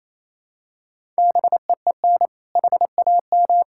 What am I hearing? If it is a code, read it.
Morse code, heard as a keyed tone, about 28 wpm, 700 Hz. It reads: BEEN HAM
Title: K1USN SST QSO Practice 28wpm